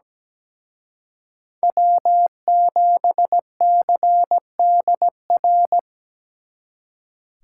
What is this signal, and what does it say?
Morse code, heard as a keyed tone, about 20 wpm, 700 Hz. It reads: W7CDR
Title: Speed Racing - USA Call Signs 17wpm